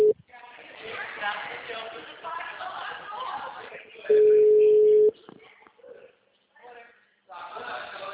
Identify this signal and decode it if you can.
Ringback tone (Europe, 425 Hz), with speech.
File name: out- -2035- - - .7